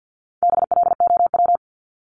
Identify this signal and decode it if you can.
Morse code, heard as a keyed tone, about 50 wpm, 700 Hz. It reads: BLOW
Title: Top 1751-2000 Words 50wpm